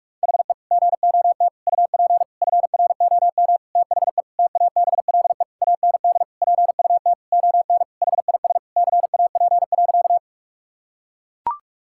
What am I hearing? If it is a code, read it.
Morse code, heard as a keyed tone, about 41 wpm, 700 Hz. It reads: HE GOT UP FROM THE TABLE AND PUT ON HIS CAP.
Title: Sentences from Top 1600 Words 45wpm